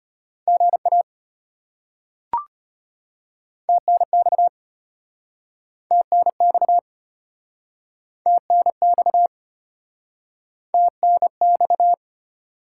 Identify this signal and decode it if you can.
Morse code, heard as a keyed tone, about 33 wpm, 700 Hz. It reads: GA TNX TNX TNX TNX